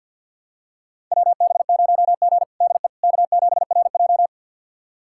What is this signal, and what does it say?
Morse code, heard as a keyed tone, about 51 wpm, 700 Hz. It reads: WB0G DE K7RJ
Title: USA Callsign DE Callsign - 4 Characters - 50wpm